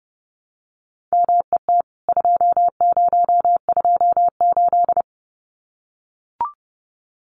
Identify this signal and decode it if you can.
Morse code, heard as a keyed tone, about 30 wpm, 700 Hz. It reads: MET 2028
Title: CWT Exchange 30wpm